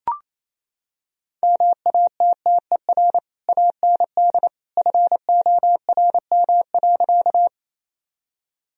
Morse code, heard as a keyed tone, about 27 wpm, 700 Hz. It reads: MATTER AND FORM.
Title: Sentences from Top 1000 Words - Review Part 1 - 28wpm